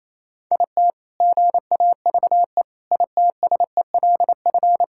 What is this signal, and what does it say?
Morse code, heard as a keyed tone, about 28 wpm, 700 Hz. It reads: IT GAVE ITSELF
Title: Sentences from Top 1000 Words - Review Part 5 - 28wpm